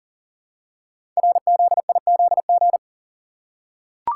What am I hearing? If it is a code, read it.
Morse code, heard as a keyed tone, about 40 wpm, 700 Hz. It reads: RZIZG
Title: Code Groups - 5 Letters - 40wpm